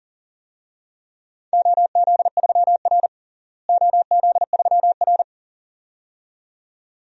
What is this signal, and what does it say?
Morse code, heard as a keyed tone, about 40 wpm, 700 Hz. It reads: OZ3R OZ3R
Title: DXCC-100 Call Signs - 4 Characters - Repeated 2x - 40wpm